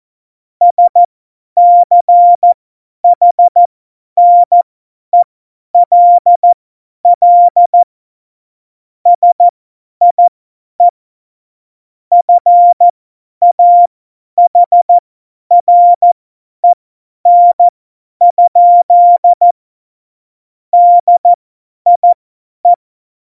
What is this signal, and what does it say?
Morse code, heard as a keyed tone, about 14 wpm, spaced as for 10 wpm, 700 Hz. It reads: SCHNELL SIE FAHREN? DIE
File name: Morsekurs / Witze / Witz 14 Juli 2020, 50 F70